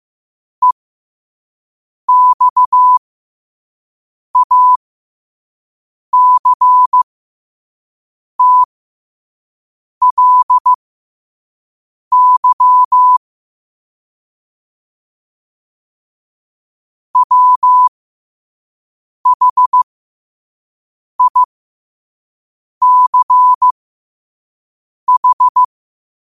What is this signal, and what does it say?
Morse code, heard as a keyed tone, about 15 wpm, spaced as for 5 wpm, 1000 Hz. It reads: EXACTLY WHICH
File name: qotdfiles / 5wpm